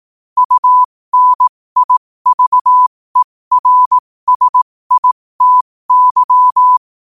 Morse code, heard as a keyed tone, about 18 wpm, spaced as for 15 wpm, 1000 Hz. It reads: UNIVERSITY
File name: qotdfiles / 15wpm